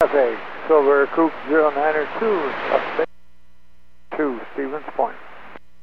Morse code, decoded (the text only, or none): none